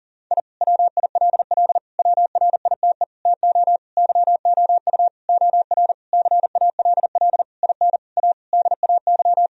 Morse code, heard as a keyed tone, about 40 wpm, 700 Hz. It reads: I WILL WRITE TO YOU OR CALL IN A DAY
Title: Sentences from Top 1000 Words - Review Part 1 - 40wpm